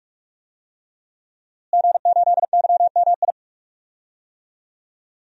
Morse code, heard as a keyed tone, about 46 wpm, 700 Hz. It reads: G8YMI
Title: DXCC-100 Call Signs - 5 Characters - Repeated 3x - 45wpm